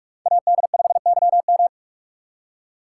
Morse code, heard as a keyed tone, about 45 wpm, 700 Hz. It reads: AD5YM